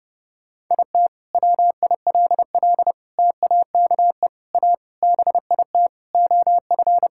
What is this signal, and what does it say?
Morse code, heard as a keyed tone, about 30 wpm, 700 Hz. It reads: IT WILL TAKE A BIT OF
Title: Sentences from Top 1000 Words - Review Part 5 - 30wpm